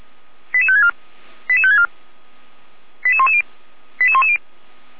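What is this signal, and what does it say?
Five-tone selective call (ZVEI), heard as quick runs of five tones, tones of about 70 ms. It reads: 88564 88564 88110 88110
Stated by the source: Five-tone ZVEI selective call 88190, received on SDR.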